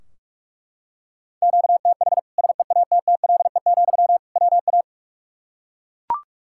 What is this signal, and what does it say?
Morse code, heard as a keyed tone, about 42 wpm, 700 Hz. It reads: QTH SEATTLE, WA
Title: Practice QSO - Exchange 1 - Beginning - 45wpm